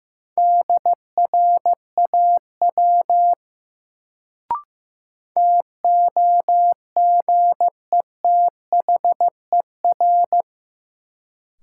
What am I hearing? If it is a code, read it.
Morse code, heard as a keyed tone, about 15 wpm, 700 Hz. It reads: DRAW TOGETHER
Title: Sentences from Top 300 Words - New Word Drill - 15wpm